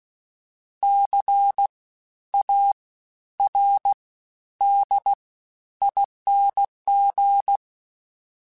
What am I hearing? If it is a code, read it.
Morse code, heard as a keyed tone, about 16 wpm, 800 Hz. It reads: C A R D ING
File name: Special INGs-mp3s / Special INGs-mp3s / Fast ing @15